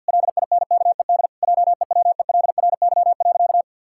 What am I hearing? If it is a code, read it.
Morse code, heard as a keyed tone, about 51 wpm, 700 Hz. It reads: LINKED JEWELRY.
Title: Pangram Challenge v2 - 50wpm